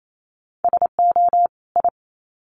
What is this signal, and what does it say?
Morse code, heard as a keyed tone, about 28 wpm, 700 Hz. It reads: SO I
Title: Sets of 2 Words 28wpm